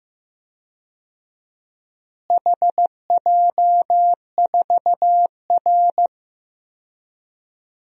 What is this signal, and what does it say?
Morse code, heard as a keyed tone, about 15 wpm, 700 Hz. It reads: HJ4R
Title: DXCC-100 Call Signs - 4 Characters - Repeated 2x - 15wpm